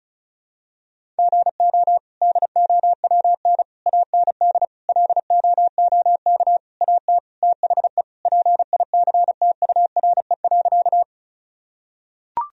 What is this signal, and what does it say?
Morse code, heard as a keyed tone, about 36 wpm, 700 Hz. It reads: GO DOWN AND LOOK AT THE PICTURE.
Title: Sentences from Top 1000 Words - Review Part 1 - 35wpm